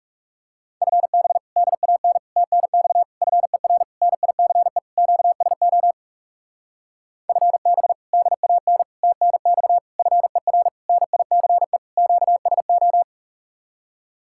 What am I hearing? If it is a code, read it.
Morse code, heard as a keyed tone, about 43 wpm, 700 Hz. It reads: FB DAN TNX FER NICE QSO FB DAN TNX FER NICE QSO